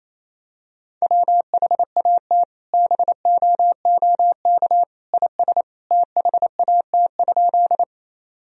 Morse code, heard as a keyed tone, about 28 wpm, 700 Hz. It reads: WHAT BOOK IS THAT?
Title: Sentences from Top 1000 Words - Review Part 1 - 28wpm